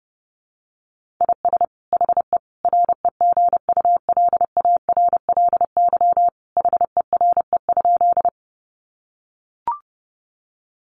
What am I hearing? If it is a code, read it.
Morse code, heard as a keyed tone, about 30 wpm, 700 Hz. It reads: IS HE REGULARLY HERE?
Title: Sentences from Top 1400 Words 30wpm